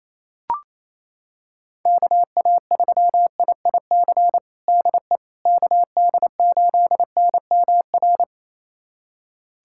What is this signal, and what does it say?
Morse code, heard as a keyed tone, about 28 wpm, 700 Hz. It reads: KA3IIC DE KD8NMR
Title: USA Callsign DE Callsign - 6 Characters - 28wpm